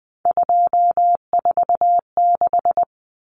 Morse code, heard as a keyed tone, about 20 wpm, 700 Hz. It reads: 246
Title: Common Cut Numbers - 3-Digit 20wpm